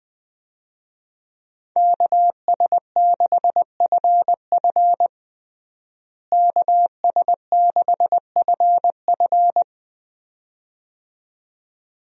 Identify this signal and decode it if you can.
Morse code, heard as a keyed tone, about 20 wpm, 700 Hz. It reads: KS6FF KS6FF
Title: USA Call Signs - 5 Characters - Repeated 2x - 20wpm